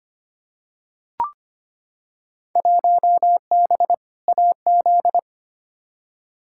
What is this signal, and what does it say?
Morse code, heard as a keyed tone, about 25 wpm, 700 Hz. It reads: 1B AZ
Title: Field Day Exchanges Encore - 25wpm